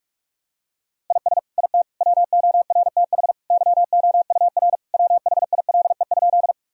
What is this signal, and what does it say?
Morse code, heard as a keyed tone, about 43 wpm, 700 Hz. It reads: IS IT WORTH YOUR WHILE?
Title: Sentences from Top 1000 Words - Review Part 1 - 45wpm